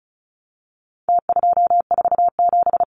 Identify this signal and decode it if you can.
Morse code, heard as a keyed tone, about 34 wpm, 700 Hz. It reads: T247
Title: Numbers - Contest Style - 4-Digit 35wpm